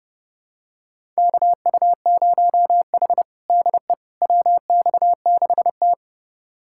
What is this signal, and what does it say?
Morse code, heard as a keyed tone, about 32 wpm, 700 Hz. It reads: KU0H DE WX6T
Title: USA Callsign DE Callsign - 4 Characters - 30wpm